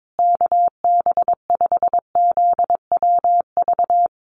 Morse code, heard as a keyed tone, about 22 wpm, 700 Hz. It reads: KB5ZWV
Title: USA Call Signs Repeated 3x 22wpm